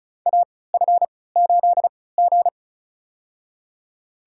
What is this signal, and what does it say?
Morse code, heard as a keyed tone, about 36 wpm, spaced as for 20 wpm, 700 Hz. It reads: AF8G
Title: USA Call Signs - 4 Characters - 3x Character Spacing - 35wpm